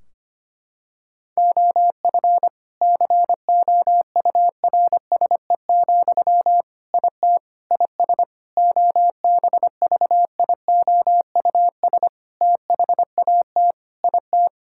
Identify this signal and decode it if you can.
Morse code, heard as a keyed tone, about 25 wpm, 700 Hz. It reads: OF COURSE, IT IS OBVIOUS THAT IT
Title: Sentences from Top 1600 Words 25wpm